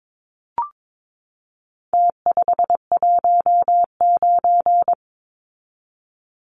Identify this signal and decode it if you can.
Morse code, heard as a keyed tone, about 22 wpm, 700 Hz. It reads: T519
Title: Numbers - Contest Style - 4-Digit 22wpm